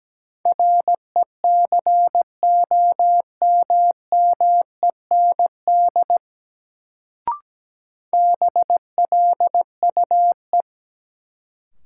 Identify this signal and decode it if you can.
Morse code, heard as a keyed tone, about 17 wpm, 700 Hz. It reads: RECOMMEND BLUE
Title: Sentences from Top 1200 Words - New Word Drill Alt - 17wpm